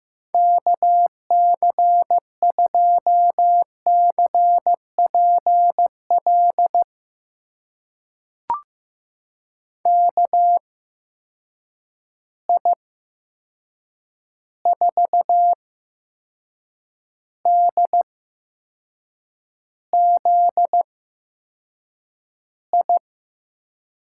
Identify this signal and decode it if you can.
Morse code, heard as a keyed tone, about 15 wpm, spaced as for 6 wpm, 700 Hz. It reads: KC2CPL KI4DZI
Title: USA Call Signs - 8x Character Spacing - 15wpm